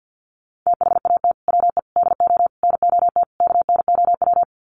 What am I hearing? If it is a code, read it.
Morse code, heard as a keyed tone, about 50 wpm, 700 Hz. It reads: THAT WE DO NOT KNOW
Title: Sets of 5 Words Encore2 50wpm